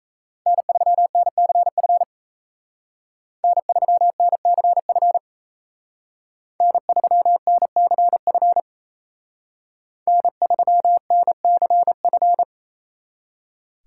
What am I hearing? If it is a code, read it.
Morse code, heard as a keyed tone, about 36 wpm, 700 Hz. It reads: N3NCF N3NCF N3NCF N3NCF